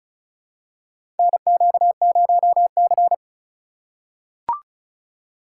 Morse code, heard as a keyed tone, about 34 wpm, 700 Hz. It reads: NQ0C
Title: USA Call Signs - 2x Character Spacing - 35wpm